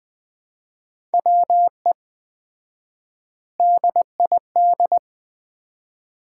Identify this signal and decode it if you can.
Morse code, heard as a keyed tone, about 20 wpm, 700 Hz. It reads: WE DID
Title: 4x Word-Spacing - Sets of 3 Words - 20wpm